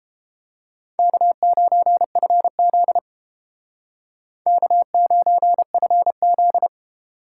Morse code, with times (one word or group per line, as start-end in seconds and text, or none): 1.0-3.0 K9FZ
4.5-6.7 K9FZ